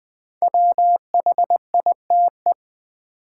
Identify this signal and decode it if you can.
Morse code, heard as a keyed tone, about 20 wpm, 700 Hz. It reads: WHITE
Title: Build A Word - 1 Letter At A Time - 5-Letter Words - 20wpm